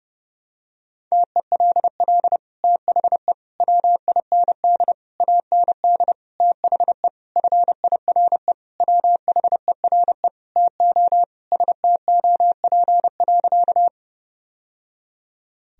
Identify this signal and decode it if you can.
Morse code, heard as a keyed tone, about 30 wpm, 700 Hz. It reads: TELL THE WIND AND THE FIRE WHERE TO STOP.
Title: Sentences from Top 1000 Words - Review Part 2 - 30wpm